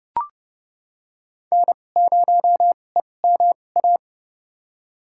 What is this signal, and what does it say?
Morse code, heard as a keyed tone, about 30 wpm, spaced as for 22 wpm, 700 Hz. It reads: N0EMA
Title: USA Call Signs - 5 Characters - 2x Character Spacing - 30wpm